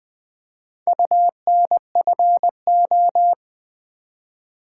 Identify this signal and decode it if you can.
Morse code, heard as a keyed tone, about 20 wpm, 700 Hz. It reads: UNFO
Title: Build A Word - 4 Letters At A Time - 11-Letter Words - 20wpm